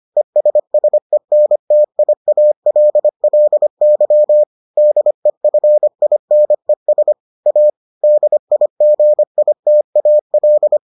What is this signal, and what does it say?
Morse code, heard as a keyed tone, about 25 wpm, 600 Hz. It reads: ESSENTIALLY DEFINES A DIGITAL